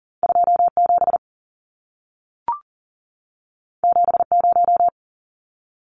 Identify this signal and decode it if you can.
Morse code, heard as a keyed tone, about 40 wpm, 700 Hz. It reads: 27 70
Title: Numbers - 2-Digit 40wpm